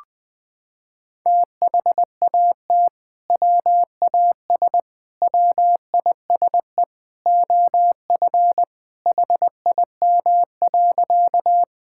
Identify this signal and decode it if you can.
Morse code, heard as a keyed tone, about 20 wpm, 700 Hz. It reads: THAT WAS WISE OF HIM.
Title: Sentences from Top 1600 Words 20wpm